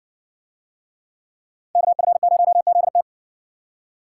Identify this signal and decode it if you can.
Morse code, heard as a keyed tone, about 54 wpm, 700 Hz. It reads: KF0ZT